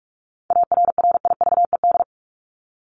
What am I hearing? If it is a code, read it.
Morse code, heard as a keyed tone, about 44 wpm, 700 Hz. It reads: ARRIVED
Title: ED Suffix - Top 100 Words with Base Words 45wpm